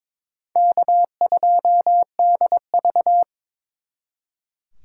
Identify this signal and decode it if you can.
Morse code, heard as a keyed tone, about 26 wpm, 700 Hz. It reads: K2DV
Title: Speed Racing - USA 1x2 Call Signs 22wpm